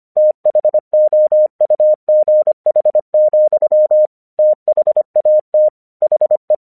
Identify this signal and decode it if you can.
Morse code, heard as a keyed tone, about 24 wpm, 610 Hz. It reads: THOUGH, THAT HE
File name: 1 English 25wpm / AQC26